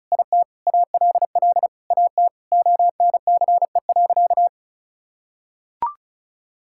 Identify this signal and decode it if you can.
Morse code, heard as a keyed tone, about 36 wpm, 700 Hz. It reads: IT ALL AT ONCE.
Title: Sentences from Top 1000 Words - Review Part 5 - 35wpm